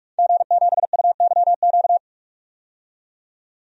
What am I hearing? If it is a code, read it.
Morse code, heard as a keyed tone, about 47 wpm, 700 Hz. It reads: G7UYQ
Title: DXCC-100 Call Signs - 5 Characters - 45wpm